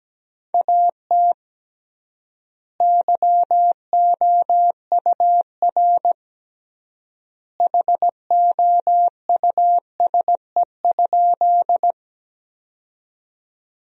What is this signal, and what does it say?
Morse code, heard as a keyed tone, about 17 wpm, 700 Hz. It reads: AT YOUR HOUSE?
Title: 3x Word-Spacing - Sentences from Top 200 Words - 17wpm